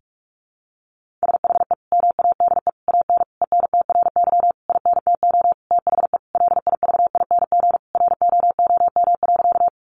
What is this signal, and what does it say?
Morse code, heard as a keyed tone, about 44 wpm, 700 Hz. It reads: SHE MADE AN ENTRY INTO THE LIVING ROOM.
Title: Sentences from Top 1500 Words 45wpm